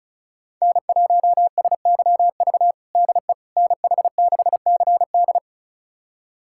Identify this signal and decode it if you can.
Morse code, heard as a keyed tone, about 34 wpm, 700 Hz. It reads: N1SYV DE NH6CD
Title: USA Callsign DE Callsign - 5 Characters - 35wpm